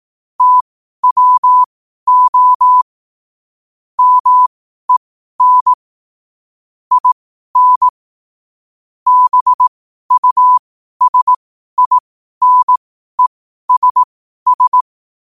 Morse code, heard as a keyed tone, about 18 wpm, spaced as for 12 wpm, 1000 Hz. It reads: TWO MEN IN BUSINESS
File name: qotdfiles / 12wpm